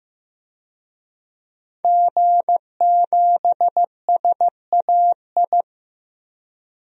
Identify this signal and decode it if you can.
Morse code, heard as a keyed tone, about 15 wpm, 700 Hz. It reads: G7SAI
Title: DXCC-100 Call Signs - 5 Characters - 15wpm